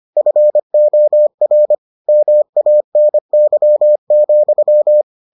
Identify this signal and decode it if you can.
Morse code, heard as a keyed tone, about 25 wpm, 600 Hz. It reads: FOR MANY,